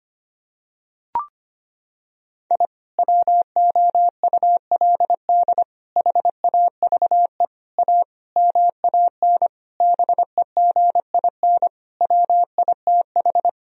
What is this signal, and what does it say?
Morse code, heard as a keyed tone, about 25 wpm, 700 Hz. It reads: I WOULD HAVE A MAN BEGIN WITH
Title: Sentences from Top 1000 Words - Review Part 2 - 25wpm